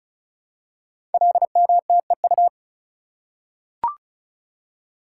Morse code, heard as a keyed tone, about 36 wpm, 690 Hz. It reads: LMTEU
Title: Code Groups - 5 Letters - 35wpm